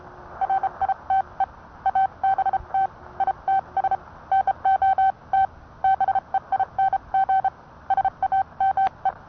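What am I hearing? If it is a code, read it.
Morse code, heard as a keyed tone, about 30 wpm, spaced as for 26 wpm, 770 Hz. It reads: RITE ABT ITS NOT BEING SAME
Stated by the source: Morse code sent by N1EA on a Vibroplex key.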